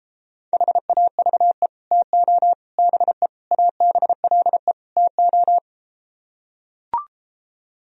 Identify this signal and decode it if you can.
Morse code, heard as a keyed tone, about 33 wpm, 700 Hz. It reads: HAVE TO BE ABLE TO